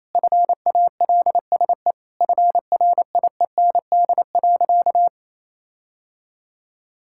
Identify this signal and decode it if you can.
Morse code, heard as a keyed tone, about 27 wpm, 700 Hz. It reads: FALSE FRIEND.
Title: Sentences from Top 1600 Words 28wpm